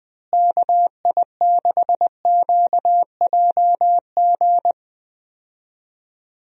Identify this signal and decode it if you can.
Morse code, heard as a keyed tone, about 20 wpm, 700 Hz. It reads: KI6QJG
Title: USA Call Signs - 6 Characters - Repeated 2x - 20wpm